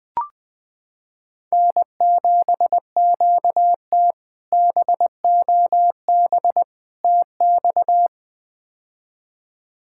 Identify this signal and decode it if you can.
Morse code, heard as a keyed tone, about 20 wpm, 700 Hz. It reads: N7QT BOB TX
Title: K1USN SST QSO Practice - 20wpm